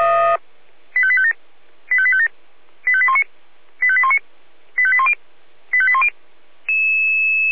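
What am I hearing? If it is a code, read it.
Five-tone selective call (ZVEI), heard as quick runs of five tones, tones of about 70 ms. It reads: 85857 85857 85819 85819 85810 85810
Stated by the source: ZVEI five-tone call 85850 received by SDR.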